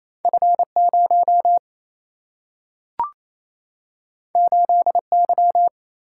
Morse code, heard as a keyed tone, about 28 wpm, 700 Hz. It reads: F0 8Y
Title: Single-Letter Single-Number 28wpm